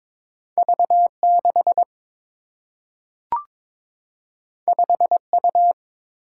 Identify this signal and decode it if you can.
Morse code, heard as a keyed tone, about 22 wpm, 700 Hz. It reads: V 6 5 U
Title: Single-Letter Single-Number 22wpm